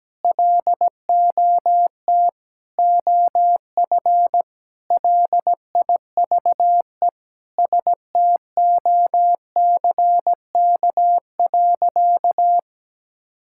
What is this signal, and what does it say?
Morse code, heard as a keyed tone, about 17 wpm, 700 Hz. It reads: LOT OF LIVE STOCK.
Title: Sentences from Top 1000 Words - Review Part 5 - 17wpm